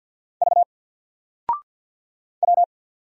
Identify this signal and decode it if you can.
Morse code, heard as a keyed tone, about 50 wpm, 700 Hz. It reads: V W
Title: Single Letter-Number 50wpm